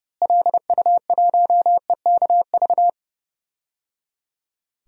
Morse code, heard as a keyed tone, about 30 wpm, 700 Hz. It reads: LU1EKV
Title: DXCC-100 Call Signs - 6 Characters - Repeated 3x - 30wpm